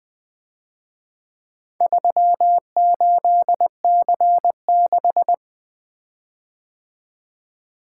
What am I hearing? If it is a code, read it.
Morse code, heard as a keyed tone, about 23 wpm, 700 Hz. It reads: 38C6
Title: Speed-Racing - Four Letter-Number Combinations - 20wpm